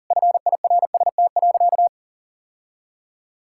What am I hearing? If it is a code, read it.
Morse code, heard as a keyed tone, about 40 wpm, 700 Hz. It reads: FIRST.
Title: Sentences from Top 1000 Words - Review Part 6 - 40wpm